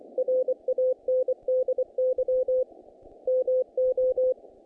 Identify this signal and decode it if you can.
Morse code, heard as a keyed tone, about 24 wpm, 510 Hz. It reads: RANDY MO
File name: contests / naqp-cw-k5tr-op-k5pi- -2100